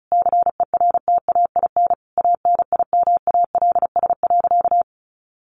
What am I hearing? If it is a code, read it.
Morse code, heard as a keyed tone, about 34 wpm, 700 Hz. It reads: CERTAIN ANIMALS.
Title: A Sound Of Thunder 35wpm